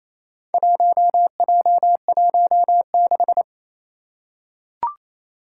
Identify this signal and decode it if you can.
Morse code, heard as a keyed tone, about 27 wpm, 700 Hz. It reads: 1J16
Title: Four Challenging Letter-Number Combinations 28wpm